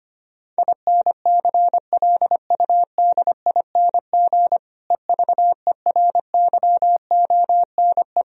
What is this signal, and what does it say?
Morse code, heard as a keyed tone, about 25 wpm, 700 Hz. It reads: INCLUDING EVERYONE
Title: Sentences from Top 1000 Words - Review Part 4 - 25wpm